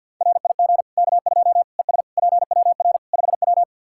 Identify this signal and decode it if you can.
Morse code, heard as a keyed tone, about 52 wpm, 700 Hz. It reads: RIG K2 ES PWR 5W
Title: Practice QSO - Exchange 2 - Info - 50wpm